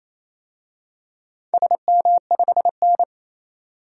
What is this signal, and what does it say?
Morse code, heard as a keyed tone, about 28 wpm, 700 Hz. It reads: SM5N
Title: DXCC-100 Call Signs - 4 Characters - 28wpm